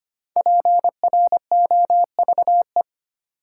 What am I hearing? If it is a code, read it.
Morse code, heard as a keyed tone, about 25 wpm, 700 Hz. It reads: PROVE
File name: 5 Letter Words - First 3 Letters Given - 25wpm